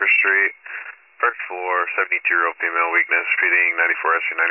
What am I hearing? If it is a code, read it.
Emergency Alert System (SAME) tones, heard as harsh buzzing bursts, with speech. It is heard from 0.6 to 4.0 s.